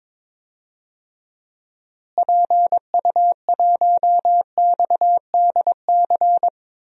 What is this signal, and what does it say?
Morse code, heard as a keyed tone, about 22 wpm, 700 Hz. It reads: PU1XDC
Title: DXCC-100 Call Signs - 6 Characters - Repeated 2x - 22wpm